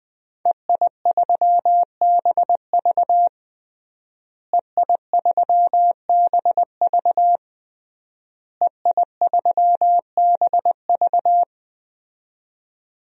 Morse code, heard as a keyed tone, about 21 wpm, 700 Hz. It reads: EI3BV EI3BV EI3BV
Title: DXCC-100 Call Signs - 5 Characters - Repeated 3x - 20wpm